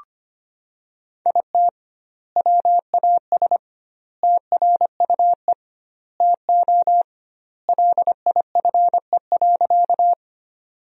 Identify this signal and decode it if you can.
Morse code, heard as a keyed tone, about 25 wpm, 700 Hz. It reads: IT WAS TRUE TO LIFE.
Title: 2x Word-Spacing - Sentences from Top 400 Words - 25wpm